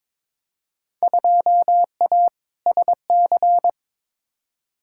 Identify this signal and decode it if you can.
Morse code, heard as a keyed tone, about 22 wpm, 700 Hz. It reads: 2A SC
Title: Field Day Exchanges - 22wpm